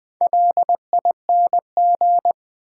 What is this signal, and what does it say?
Morse code, heard as a keyed tone, about 20 wpm, 700 Hz. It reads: LING
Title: Build A Word - 2 Letters At A Time - 6-Letter Words - 20wpm